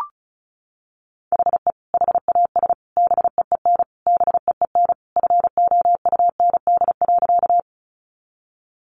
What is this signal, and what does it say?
Morse code, heard as a keyed tone, about 34 wpm, 700 Hz. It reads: HE HAS BEEN BEEN FOUND.
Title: Sentences from Top 300 Words 35wpm